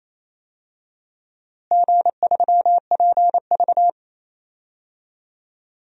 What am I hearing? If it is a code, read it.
Morse code, heard as a keyed tone, about 28 wpm, 700 Hz. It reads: G3PV
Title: DXCC-100 Call Signs - 4 Characters - Repeated 3x - 28wpm